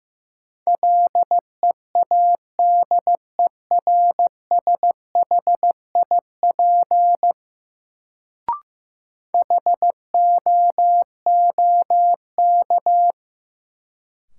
Morse code, heard as a keyed tone, about 15 wpm, 700 Hz. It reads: LEADERSHIP HOOK
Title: Sentences from Top 1600 Words - New Word Drill Alt - 15wpm